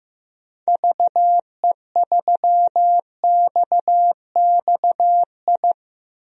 Morse code, heard as a keyed tone, about 15 wpm, 700 Hz. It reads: VE3XXI
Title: DXCC-100 Call Signs - 6 Characters - 15wpm